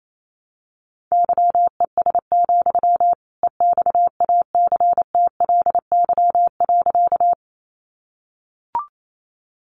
Morse code, heard as a keyed tone, about 28 wpm, 700 Hz. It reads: YES, EXACTLY.
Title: Sentences from Top 900 Words 28wpm